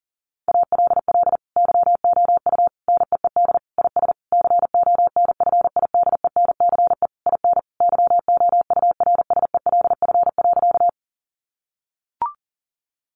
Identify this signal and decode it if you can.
Morse code, heard as a keyed tone, about 40 wpm, 700 Hz. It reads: ALL YOU NEED IS CONFIDENCE IN YOURSELF.
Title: Sentences from Top 1500 Words 40wpm